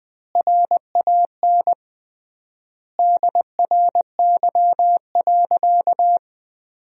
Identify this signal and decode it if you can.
Morse code, heard as a keyed tone, about 20 wpm, 700 Hz. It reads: RAN DRY.
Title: 3x Word-Spacing - Sentences from Top 500 Words - 20wpm